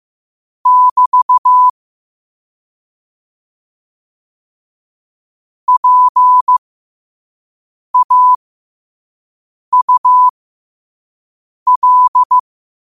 Morse code, heard as a keyed tone, about 15 wpm, spaced as for 5 wpm, 1000 Hz. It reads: <BT> PAUL
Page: New quotes every day in morse code at 5 Words per minute.